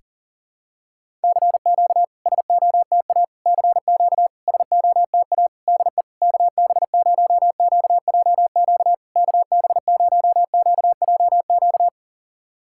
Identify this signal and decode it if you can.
Morse code, heard as a keyed tone, about 40 wpm, 690 Hz. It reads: CQ SOTA CQ SOTA DE KB0QJQ KB0QJQ
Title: SOTA QSO Practice 40wpm